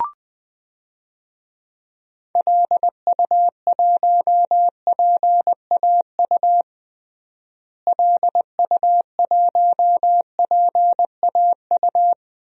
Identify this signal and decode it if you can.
Morse code, heard as a keyed tone, about 20 wpm, 700 Hz. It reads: LU1PAU LU1PAU
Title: DXCC-100 Call Signs - 6 Characters - Repeated 2x - 20wpm